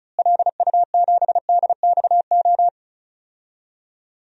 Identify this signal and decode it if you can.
Morse code, heard as a keyed tone, about 36 wpm, 700 Hz. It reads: LU7DXO
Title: DXCC-100 Call Signs - 6 Characters - Repeated 2x - 35wpm